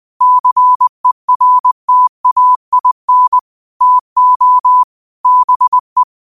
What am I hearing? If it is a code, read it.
Morse code, heard as a keyed tone, about 20 wpm, 1000 Hz. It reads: CERTAIN TO BE
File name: qotdfiles / 20wpm